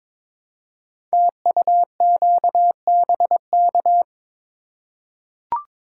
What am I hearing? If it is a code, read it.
Morse code, heard as a keyed tone, about 22 wpm, 690 Hz. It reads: TUQBK
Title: Code Groups - 5 Letters - 22wpm